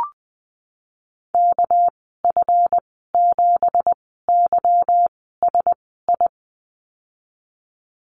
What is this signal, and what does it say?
Morse code, heard as a keyed tone, about 20 wpm, spaced as for 15 wpm, 700 Hz. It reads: KF7YSI
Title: 207 Lesson 33g - Letter j - Call Signs 20wpm